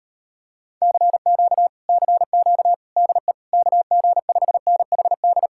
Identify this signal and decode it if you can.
Morse code, heard as a keyed tone, about 33 wpm, 700 Hz. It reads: CQ CQ DE KG5NHD